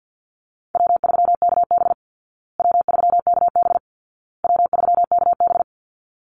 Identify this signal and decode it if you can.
Morse code, heard as a keyed tone, about 48 wpm, 700 Hz. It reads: W3KB W3KB W3KB
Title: Easy Call Signs Repeated 3x - 50wpm